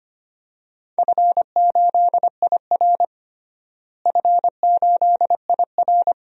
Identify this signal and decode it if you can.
Morse code, heard as a keyed tone, about 25 wpm, 700 Hz. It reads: F8IR F8IR
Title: DXCC-100 Call Signs - 4 Characters - Repeated 2x - 25wpm